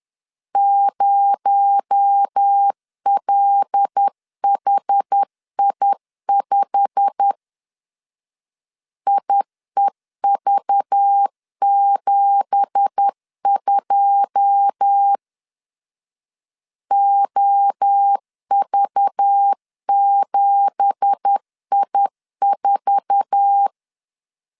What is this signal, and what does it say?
Morse code, heard as a keyed tone, about 10 wpm, 800 Hz. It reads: 0LHI5 IEV72 OV7I4
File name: CW / DET / 1 9 1500C S A 10wpm 4511